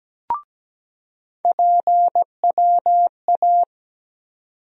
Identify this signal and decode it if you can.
Morse code, heard as a keyed tone, about 17 wpm, 700 Hz. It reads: PWA
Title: USA Call Sign Postfixes - 3 Letters 17wpm